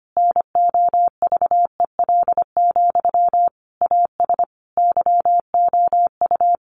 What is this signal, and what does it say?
Morse code, heard as a keyed tone, about 25 wpm, 700 Hz. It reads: NOVEL, AS YOU
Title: Sentences from Top 1300 Words 25wpm